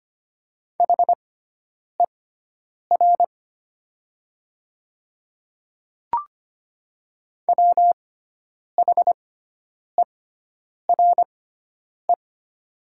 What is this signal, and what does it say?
Morse code, heard as a keyed tone, about 25 wpm, spaced as for 9 wpm, 700 Hz. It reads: HER WHERE
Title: 6x Character Spacing - Top 300 Words - 25wpm